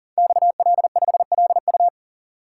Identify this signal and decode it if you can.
Morse code, heard as a keyed tone, about 39 wpm, 700 Hz. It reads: XL5LU
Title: Code Groups - 5 Letter:Numbers - 40wpm